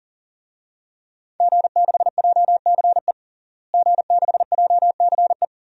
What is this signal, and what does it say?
Morse code, heard as a keyed tone, about 40 wpm, 700 Hz. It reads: G6JCE G6JCE
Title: DXCC-100 Call Signs - 5 Characters - Repeated 2x - 40wpm